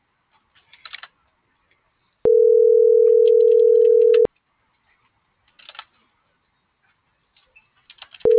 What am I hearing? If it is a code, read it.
Ringback tone (North America), 440 + 480 Hz.